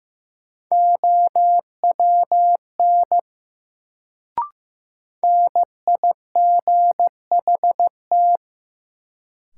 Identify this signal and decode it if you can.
Morse code, heard as a keyed tone, about 16 wpm, 700 Hz. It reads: OWN NIGHT
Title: Sentences from Top 300 Words - New Word Drill - 15wpm